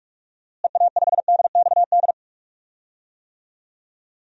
Morse code, heard as a keyed tone, about 42 wpm, 700 Hz. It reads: EA5DXD
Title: DXCC-100 Call Signs - 6 Characters - Repeated 3x - 45wpm